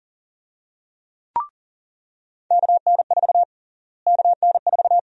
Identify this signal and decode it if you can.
Morse code, heard as a keyed tone, about 40 wpm, 700 Hz. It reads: KN4 KN4
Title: USA Call Sign Prefixes - Repeated 2x - 40wpm